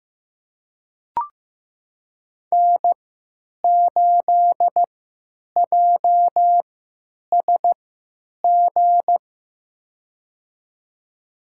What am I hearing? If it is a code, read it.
Morse code, heard as a keyed tone, about 15 wpm, spaced as for 9 wpm, 700 Hz. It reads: N8JSG
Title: USA Call Signs - 5 Characters - 3x Character Spacing - 15wpm